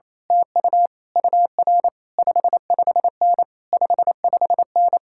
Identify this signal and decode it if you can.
Morse code, heard as a keyed tone, about 27 wpm, 700 Hz. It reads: TU UR 55N 55N
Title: Speed-Racing Common QSO Phrases - 28wpm